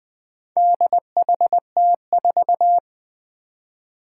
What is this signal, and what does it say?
Morse code, heard as a keyed tone, about 20 wpm, 700 Hz. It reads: DHT4
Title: Four Challenging Letter-Number Combinations 20wpm